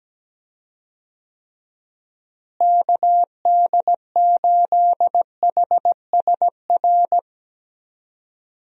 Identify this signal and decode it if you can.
Morse code, heard as a keyed tone, about 17 wpm, 700 Hz. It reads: KD8HSR
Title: USA Call Signs - 6 Characters - Repeated 2x - 17wpm